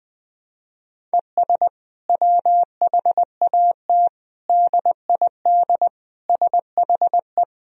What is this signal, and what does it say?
Morse code, heard as a keyed tone, about 21 wpm, 700 Hz. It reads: ES WHAT DID SHE
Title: Sentences from Top 200 Words and Abbreviations 20wpm